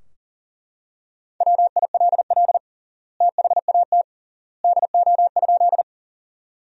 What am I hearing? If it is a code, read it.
Morse code, heard as a keyed tone, about 40 wpm, 700 Hz. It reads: WILL THAT DO?
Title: 3x Word-Spacing - Sentences from Top 100 Words - 40wpm